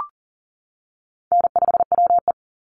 Morse code, heard as a keyed tone, about 40 wpm, 700 Hz. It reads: N5WE
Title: Easy Call Signs - 40wpm